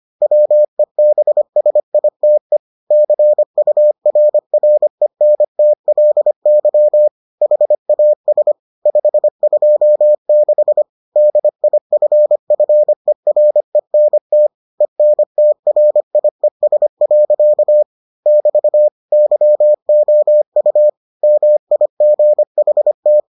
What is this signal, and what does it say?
Morse code, heard as a keyed tone, about 25 wpm, 600 Hz. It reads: WEBSITE CURRENTLY HAS 526 DIFFERENT ENTRIES. <BT> YOU MIGHT